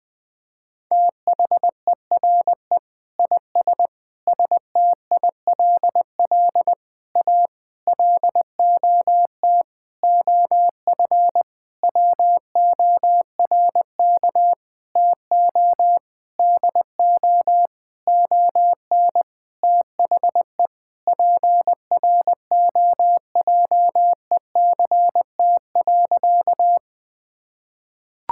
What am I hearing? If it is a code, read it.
Morse code, heard as a keyed tone, about 20 wpm, 700 Hz. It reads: THERE IS STILL A LOT OF WORK TO DO ON THE PROJECT.
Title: Sentences with 9-Word Phrases - 20wpm